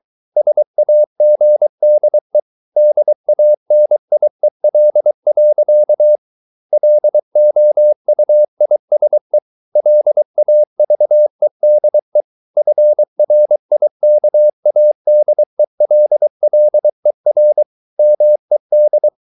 Morse code, heard as a keyed tone, about 23 wpm, 600 Hz. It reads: SAGDE DANIEL. LOUISE LAVEDE FRIKADELLER MED
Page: Ferie 23 wpm